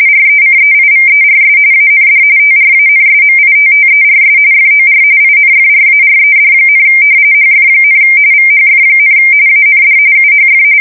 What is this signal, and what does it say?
RTTY, about 45.45 baud. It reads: RECEIVERS AND TRANSMITTERS FOR INFRARED, HF, VHF, AND UHF
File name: fsk170 / battleshipnj